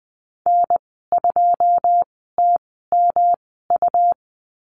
Morse code, heard as a keyed tone, about 20 wpm, spaced as for 14 wpm, 700 Hz. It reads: N2TMU
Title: 161 Lesson 26g - Letter g - Call Signs 20wpm